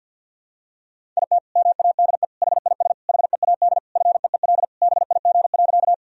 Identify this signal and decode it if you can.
Morse code, heard as a keyed tone, about 47 wpm, 700 Hz. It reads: IT MADE HIS HEAD FEEL BIG.
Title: Sentences from Top 1000 Words - Review Part 1 - 50wpm